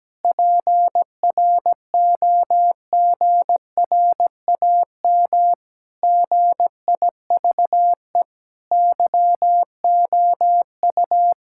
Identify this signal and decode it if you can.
Morse code, heard as a keyed tone, about 17 wpm, 700 Hz. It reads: PROGRAM GIVE YOU
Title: Sentences from Top 1000 Words - Review Part 5 - 17wpm